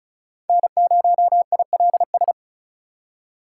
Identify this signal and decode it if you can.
Morse code, heard as a keyed tone, about 34 wpm, 700 Hz. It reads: N0ILS
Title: USA Call Signs - 5 Characters - 4x Character Spacing - 35wpm